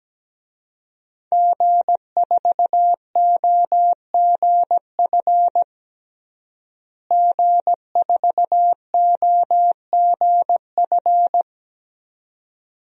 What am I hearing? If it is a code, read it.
Morse code, heard as a keyed tone, about 17 wpm, 700 Hz. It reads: G4OGF G4OGF
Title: DXCC-100 Call Signs - 5 Characters - Repeated 2x - 17wpm